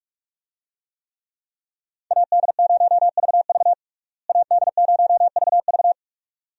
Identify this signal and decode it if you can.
Morse code, heard as a keyed tone, about 42 wpm, 700 Hz. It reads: AD0VV AD0VV
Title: USA Call Signs - 5 Characters - Repeated 2x - 45wpm